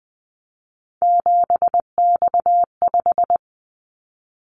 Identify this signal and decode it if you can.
Morse code, heard as a keyed tone, about 20 wpm, 700 Hz. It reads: 7X5
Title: Callsign Prefixes for Top 100 DXCC - 20wpm